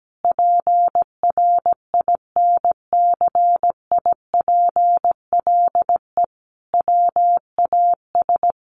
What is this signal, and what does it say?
Morse code, heard as a keyed tone, about 17 wpm, 700 Hz. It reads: PRINCIPLE WAS
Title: Sentences from Top 1400 Words 17wpm